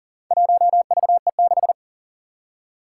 Morse code, heard as a keyed tone, about 38 wpm, 700 Hz. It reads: 1VE6
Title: Four Challenging Letter-Number Combinations 40wpm